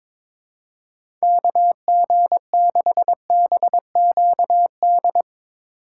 Morse code, heard as a keyed tone, about 22 wpm, 700 Hz. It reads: KG6BQD
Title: USA Call Signs - 6 Characters - 3x Character Spacing - 22wpm